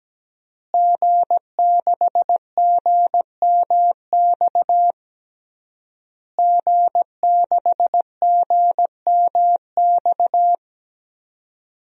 Morse code, heard as a keyed tone, about 17 wpm, 700 Hz. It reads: G6GMX G6GMX
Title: DXCC-100 Call Signs - 5 Characters - Repeated 3x - 17wpm